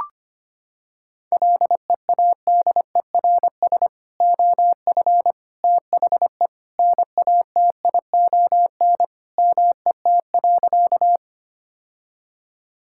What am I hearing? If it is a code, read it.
Morse code, heard as a keyed tone, about 25 wpm, 700 Hz. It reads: LEADERS OF THE NATION MET.
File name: practice / 25wpm / Sentences with 3-Word Phrases - 25wpm